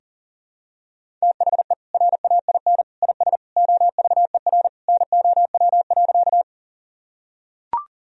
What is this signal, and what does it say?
Morse code, heard as a keyed tone, about 40 wpm, 700 Hz. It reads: THE RAIN IS OVER NOW.
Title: Sentences from Top 1000 Words - Review Part 1 - 40wpm